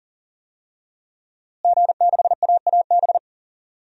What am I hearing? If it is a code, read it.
Morse code, heard as a keyed tone, about 40 wpm, 700 Hz. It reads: G6AAB
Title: DXCC-100 Call Signs - 5 Characters - Repeated 2x - 40wpm